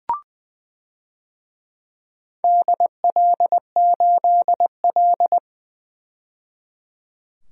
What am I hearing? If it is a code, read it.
Morse code, heard as a keyed tone, about 20 wpm, 700 Hz. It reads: DL8L
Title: DXCC-100 Call Signs - 4 Characters - 20wpm